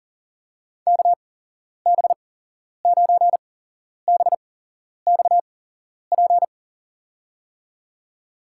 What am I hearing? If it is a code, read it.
Morse code, heard as a keyed tone, about 37 wpm, spaced as for 11 wpm, 700 Hz. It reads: KB9BXP
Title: USA Call Signs - 8x Character Spacing - 40wpm